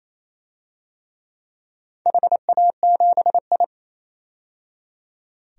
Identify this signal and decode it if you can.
Morse code, heard as a keyed tone, about 27 wpm, 700 Hz. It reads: HA7I
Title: DXCC-100 Call Signs - 4 Characters - 28wpm